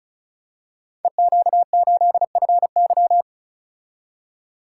Morse code, heard as a keyed tone, about 36 wpm, 700 Hz. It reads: EQ8FY
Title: Code Groups - 5 Letter:Numbers - 35wpm